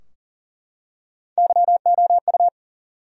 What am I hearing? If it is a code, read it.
Morse code, heard as a keyed tone, about 40 wpm, 700 Hz. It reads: YOU
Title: Top 100 Words Repeated 40wpm